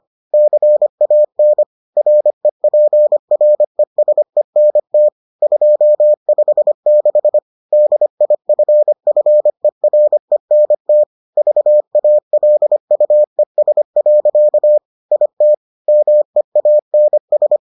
Morse code, heard as a keyed tone, about 25 wpm, 600 Hz. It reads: CAN REPRESENT 256 DIFFERENT VALUES. IT MEANS